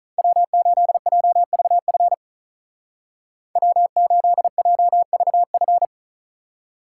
Morse code, heard as a keyed tone, about 45 wpm, 700 Hz. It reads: W8JVF W8JVF